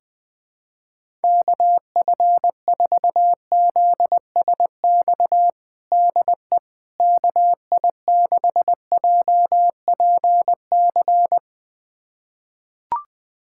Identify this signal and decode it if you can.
Morse code, heard as a keyed tone, about 20 wpm, 700 Hz. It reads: KF4ZSX DE KI6JPC
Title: USA Callsign DE Callsign 20wpm